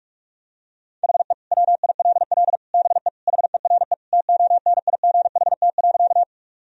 Morse code, heard as a keyed tone, about 46 wpm, 700 Hz. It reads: HE WILL BE HERE TONIGHT.
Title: Sentences from Top 1000 Words - Review Part 3 - 45wpm